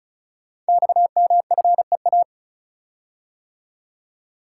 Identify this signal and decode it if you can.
Morse code, heard as a keyed tone, about 36 wpm, 700 Hz. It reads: XMFEA
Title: Code Groups - 5 Letter:Numbers - 35wpm